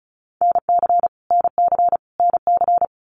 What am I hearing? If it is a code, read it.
Morse code, heard as a keyed tone, about 34 wpm, 700 Hz. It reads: NC NC NC
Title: US States Repeated 3x 35wpm